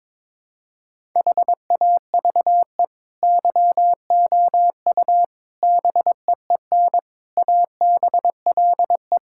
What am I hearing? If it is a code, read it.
Morse code, heard as a keyed tone, about 22 wpm, 700 Hz. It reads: HAVE YOU BEEN ABLE
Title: Sentences from Top 1000 Words - Review Part 6 - 22wpm